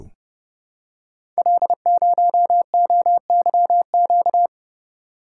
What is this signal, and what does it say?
Morse code, heard as a keyed tone, about 30 wpm, 700 Hz. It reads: L0OYQ
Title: Code Groups - 5 Letter:Numbers - 30wpm